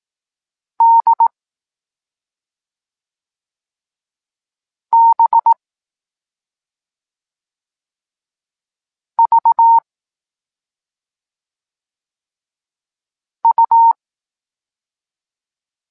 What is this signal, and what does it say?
Morse code, heard as a keyed tone, about 18 wpm, 920 Hz. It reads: D B V U
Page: Tekst 14 – doskonalenie odbioru poznanych liter